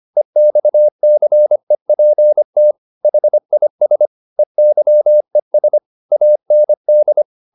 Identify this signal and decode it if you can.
Morse code, heard as a keyed tone, about 25 wpm, 600 Hz. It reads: EXCEPT HIS EYES AND
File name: the old man and the sea part 1 25wpm